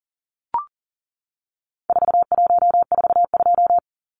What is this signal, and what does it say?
Morse code, heard as a keyed tone, about 40 wpm, 700 Hz. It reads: 4142
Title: Numbers - 4-Digit 40wpm